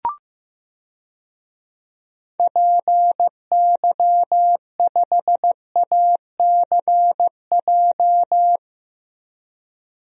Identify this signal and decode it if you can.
Morse code, heard as a keyed tone, about 16 wpm, 700 Hz. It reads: PY5ACJ
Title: DXCC-100 Call Signs - 6 Characters - 15wpm